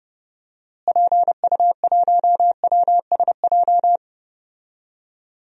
Morse code, heard as a keyed tone, about 32 wpm, 700 Hz. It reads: PU1WSJ
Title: DXCC-100 Call Signs - 6 Characters - Repeated 3x - 30wpm